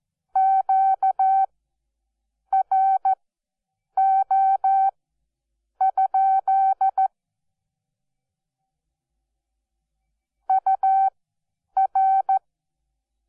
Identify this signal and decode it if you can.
Morse code, heard as a keyed tone, about 14 wpm, spaced as for 7 wpm, 770 Hz. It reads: QRO? UR
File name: Onlinelehrgang / morsen / DJ4UF Morsen12